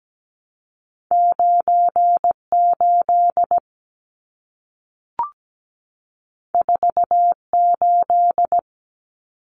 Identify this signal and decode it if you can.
Morse code, heard as a keyed tone, about 17 wpm, 700 Hz. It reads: 98 48
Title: Numbers - 2-Digit 17wpm